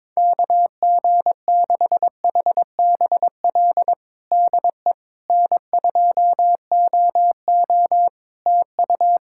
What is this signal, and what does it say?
Morse code, heard as a keyed tone, about 22 wpm, 700 Hz. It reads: KG6HBL DE N2OO TU
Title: K1USN SST QSO Practice 22wpm